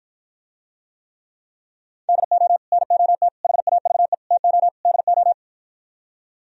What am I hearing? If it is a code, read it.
Morse code, heard as a keyed tone, about 52 wpm, 700 Hz. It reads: DO NOT HAVE TO DO